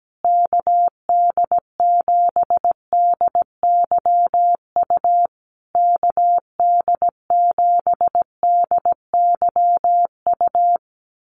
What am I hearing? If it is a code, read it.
Morse code, heard as a keyed tone, about 17 wpm, 700 Hz. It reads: KD7DYU KD7DYU
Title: USA Call Signs Repeated 3x 17wpm